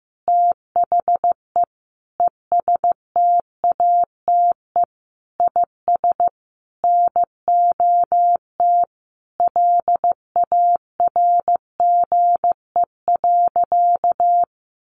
Morse code, heard as a keyed tone, about 15 wpm, 700 Hz. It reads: THE ESTATE IS NOT LARGE.
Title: Sentences from Top 1200 Words 15wpm